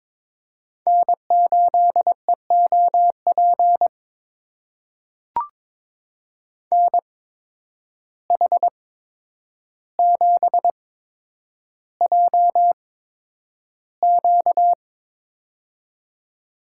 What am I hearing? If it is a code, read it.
Morse code, heard as a keyed tone, about 21 wpm, spaced as for 9 wpm, 700 Hz. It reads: N8EOP NH7JQ
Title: USA Call Signs - 5 Characters - 8x Character Spacing - 22wpm